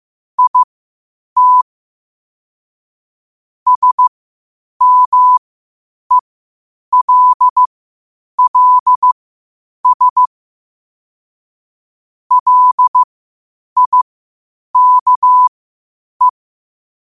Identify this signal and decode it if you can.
Morse code, heard as a keyed tone, about 15 wpm, spaced as for 8 wpm, 1000 Hz. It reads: IT SMELLS LIKE
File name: qotdfiles / 8wpm